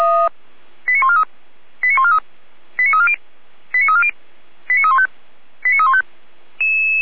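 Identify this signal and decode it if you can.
Five-tone selective call (ZVEI), heard as quick runs of five tones, tones of about 70 ms. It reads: 89143 89143 89340 89340 89315 89315
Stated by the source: ZVEI five-tone call received on SDR, fire brigade Orlamünde.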